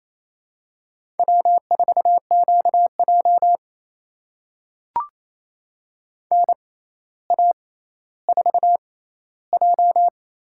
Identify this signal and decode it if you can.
Morse code, heard as a keyed tone, about 28 wpm, spaced as for 14 wpm, 700 Hz. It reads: W4QJ NA4J
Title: USA Call Signs - 4 Characters - 6x Character Spacing - 28wpm